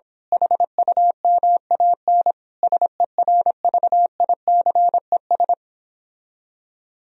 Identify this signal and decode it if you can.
Morse code, heard as a keyed tone, about 26 wpm, 700 Hz. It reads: HUMAN SERVICES
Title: Speed-Racing - Sets of 5 Words 22wpm